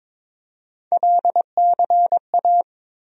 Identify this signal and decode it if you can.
Morse code, heard as a keyed tone, about 22 wpm, 700 Hz. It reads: LCA
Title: USA Call Sign Postfixes - 3 Letters 22wpm